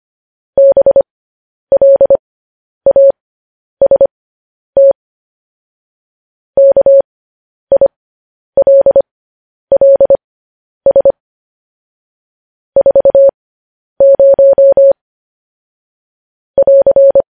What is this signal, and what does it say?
Morse code, heard as a keyed tone, about 25 wpm, spaced as for 10 wpm, 550 Hz. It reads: BLAST KILLS 40 <AR>
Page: news headlines in morse code at 10 wpm